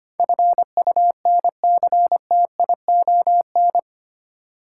Morse code, heard as a keyed tone, about 25 wpm, 700 Hz. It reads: FUNCTION
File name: Eight Letter Words - 25wpm